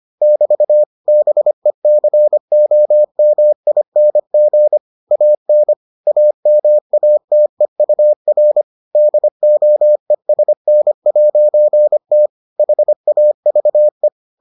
Morse code, heard as a keyed tone, about 25 wpm, 600 Hz. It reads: <BT> BECOMING AN AMATEUR DOESN'T HAVE